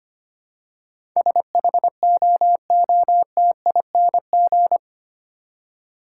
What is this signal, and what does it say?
Morse code, heard as a keyed tone, about 25 wpm, 700 Hz. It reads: SHOOTING
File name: Eight Letter Words - 25wpm